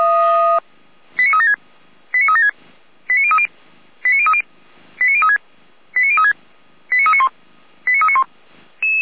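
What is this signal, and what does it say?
Five-tone selective call (ZVEI), heard as quick runs of five tones, tones of about 70 ms. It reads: 89376 89376 89030 89030 89036 89036 89391 89391
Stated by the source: ZVEI five-tone selective call received via SDR.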